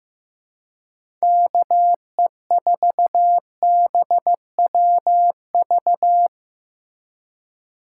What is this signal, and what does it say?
Morse code, heard as a keyed tone, about 15 wpm, 700 Hz. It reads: KE4BWV
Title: USA Call Signs - 6 Characters - 2x Character Spacing - 15wpm